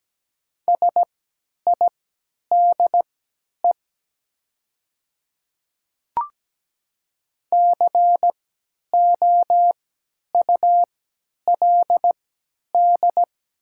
Morse code, heard as a keyed tone, about 17 wpm, spaced as for 10 wpm, 700 Hz. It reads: SIDE COULD
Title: 3x Character Spacing - Top 200 Words - 17wpm